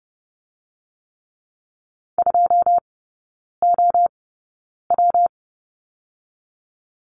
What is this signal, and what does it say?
Morse code, heard as a keyed tone, about 30 wpm, 700 Hz. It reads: 2 O W
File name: Lesson 19e - Letter w - Three Characters 30wpm